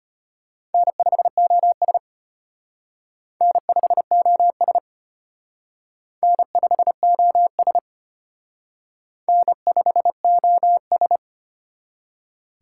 Morse code, heard as a keyed tone, about 32 wpm, 700 Hz. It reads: N5OS N5OS N5OS N5OS